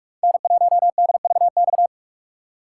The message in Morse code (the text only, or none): N1DVX